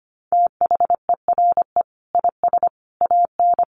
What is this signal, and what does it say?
Morse code, heard as a keyed tone, about 25 wpm, 700 Hz. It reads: THERE IS AN
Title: Sentences from Top 1400 Words 25wpm